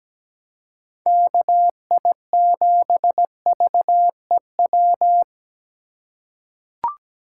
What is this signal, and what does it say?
Morse code, heard as a keyed tone, about 17 wpm, 700 Hz. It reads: KI7VEW
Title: USA Call Signs - 6 Characters - 3x Character Spacing - 17wpm